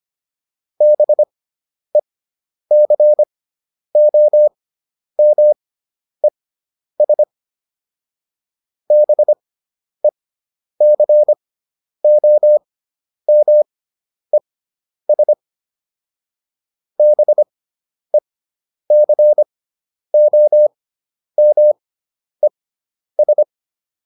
Morse code, heard as a keyed tone, about 25 wpm, spaced as for 10 wpm, 600 Hz. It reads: BECOMES BECOMES BECOMES